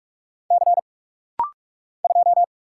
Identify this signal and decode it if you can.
Morse code, heard as a keyed tone, about 44 wpm, 700 Hz. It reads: C 2
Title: Single Letter-Number Alt - Rapid-Fire 45wpm